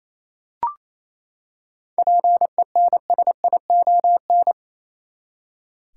Morse code, heard as a keyed tone, about 28 wpm, 700 Hz. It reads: PENSION
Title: Seven Letter Words - 28wpm